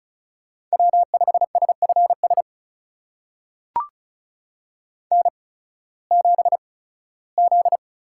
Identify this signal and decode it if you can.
Morse code, heard as a keyed tone, about 36 wpm, spaced as for 19 wpm, 700 Hz. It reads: W5SFS N7Z
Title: USA Call Signs - 5 Characters - 8x Character Spacing - 35wpm